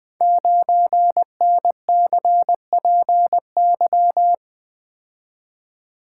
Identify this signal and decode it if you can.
Morse code, heard as a keyed tone, about 20 wpm, 700 Hz. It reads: 9NCPY
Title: Code Groups - 5 Letter:Numbers - 20wpm